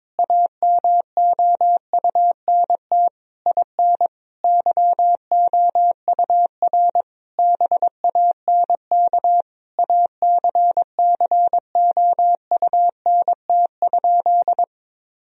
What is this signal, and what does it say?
Morse code, heard as a keyed tone, about 22 wpm, 700 Hz. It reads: AMOUNT IN YOUR BANK ACCOUNT?
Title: Sentences from Top 1000 Words - Review Part 4 - 22wpm